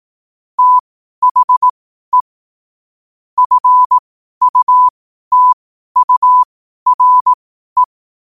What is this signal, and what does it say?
Morse code, heard as a keyed tone, about 18 wpm, spaced as for 12 wpm, 1000 Hz. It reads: THE FUTURE
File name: qotdfiles / 12wpm